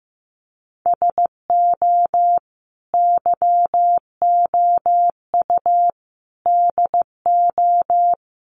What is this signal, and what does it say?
Morse code, heard as a keyed tone, about 15 wpm, 700 Hz. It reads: SO YOU DO
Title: Sets of 5 Words Encore 15wpm